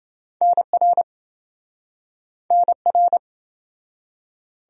Morse code, heard as a keyed tone, about 28 wpm, 700 Hz. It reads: NR NR